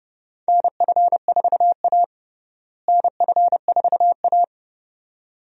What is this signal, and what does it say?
Morse code, heard as a keyed tone, about 30 wpm, 700 Hz. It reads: NF4A NF4A
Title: USA Call Signs - 4 Characters - Repeated 2x - 30wpm